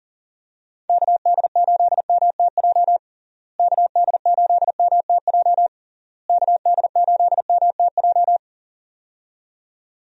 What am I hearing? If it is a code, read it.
Morse code, heard as a keyed tone, about 40 wpm, 700 Hz. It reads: KD8MTJ KD8MTJ KD8MTJ
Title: USA Call Signs - 6 Characters - Repeated 3x - 40wpm